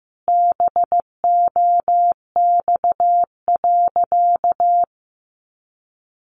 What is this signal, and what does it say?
Morse code, heard as a keyed tone, about 15 wpm, 700 Hz. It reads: BOX.
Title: Sentences from Top 500 Words 15wpm